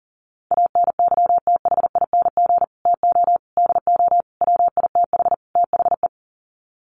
Morse code, heard as a keyed tone, about 40 wpm, 700 Hz. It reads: ANYTHING TO DO WITH THE
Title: Sets of 5 Words Encore 40wpm